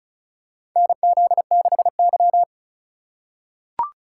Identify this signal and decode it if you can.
Morse code, heard as a keyed tone, about 36 wpm, 700 Hz. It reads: NZ6Y
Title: USA Call Signs - 4 Characters - 3x Character Spacing - 35wpm